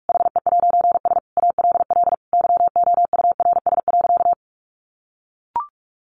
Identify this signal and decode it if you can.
Morse code, heard as a keyed tone, about 45 wpm, 700 Hz. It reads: HE'S ALL YOURS.
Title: A Sound Of Thunder 45wpm